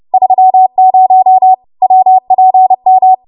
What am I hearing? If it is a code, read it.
Morse code, heard as a keyed tone, about 30 wpm, 750 Hz. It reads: 30 WPM
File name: Morse / 30WPM